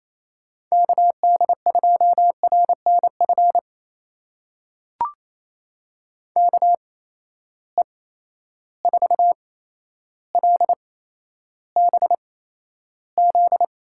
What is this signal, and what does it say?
Morse code, heard as a keyed tone, about 28 wpm, spaced as for 12 wpm, 700 Hz. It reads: KD2RNF KE4LBZ
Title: USA Call Signs - 6 Characters - 8x Character Spacing - 28wpm